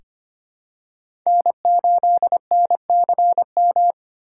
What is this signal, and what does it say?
Morse code, heard as a keyed tone, about 25 wpm, 700 Hz. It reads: N8NCM
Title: USA Call Signs - 2x Character Spacing - 25wpm